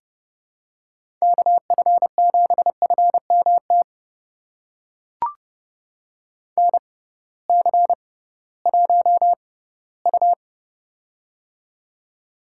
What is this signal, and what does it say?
Morse code, heard as a keyed tone, about 30 wpm, spaced as for 17 wpm, 700 Hz. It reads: KF7FMT NC1U
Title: USA Call Signs - 6x Character Spacing - 30wpm